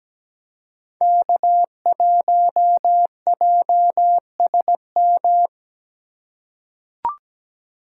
Morse code, heard as a keyed tone, about 17 wpm, 700 Hz. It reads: K1JSM
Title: USA Call Signs - 3x Character Spacing - 17wpm